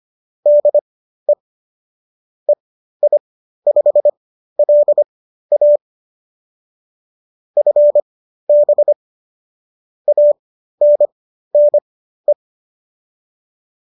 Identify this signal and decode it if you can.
Morse code, heard as a keyed tone, about 25 wpm, spaced as for 13 wpm, 570 Hz. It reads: DE EI5LA FB ANNE